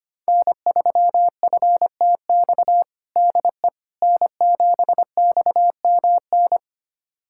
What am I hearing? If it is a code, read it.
Morse code, heard as a keyed tone, about 25 wpm, 700 Hz. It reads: N3FTX DE N7XMN
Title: USA Callsign DE Callsign - 5 Characters - 25wpm